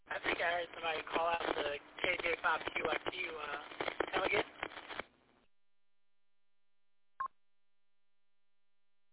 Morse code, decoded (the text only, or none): none